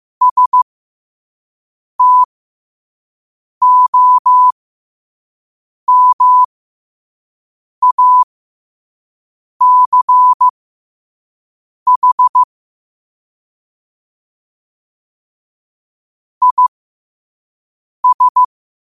Morse code, heard as a keyed tone, about 15 wpm, spaced as for 5 wpm, 1000 Hz. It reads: STOMACH IS
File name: qotdfiles / 5wpm